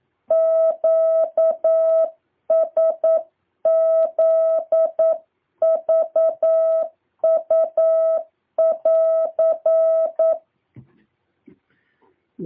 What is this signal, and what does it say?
Morse code, heard as a keyed tone, about 9 wpm, 650 Hz. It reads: QSZVU<AR>
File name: jh8cbh / 2023cw